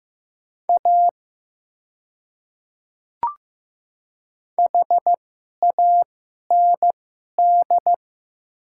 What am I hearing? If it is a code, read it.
Morse code, heard as a keyed tone, about 15 wpm, spaced as for 11 wpm, 700 Hz. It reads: A HAND
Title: 2x Character Spacing - Top 500 Words - 15wpm